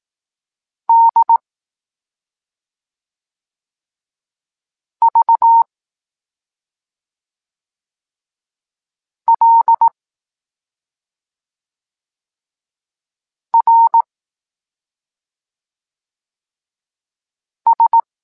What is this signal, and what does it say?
Morse code, heard as a keyed tone, about 18 wpm, 920 Hz. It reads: D V L R S
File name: Tekst-26 1